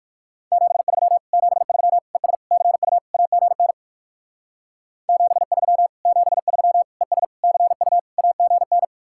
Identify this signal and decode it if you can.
Morse code, heard as a keyed tone, about 58 wpm, 700 Hz. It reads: 73 73 ES CU AGN 73 73 ES CU AGN